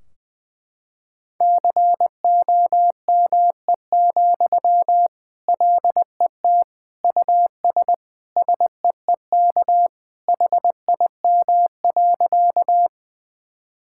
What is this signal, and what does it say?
Morse code, heard as a keyed tone, about 19 wpm, 700 Hz. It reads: COME, LET US SEEK HIM.
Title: Sentences from Top 1600 Words 20wpm